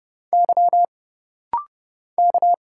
Morse code, heard as a keyed tone, about 29 wpm, 700 Hz. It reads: Y K
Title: Single Letters - Rapid-Fire 30wpm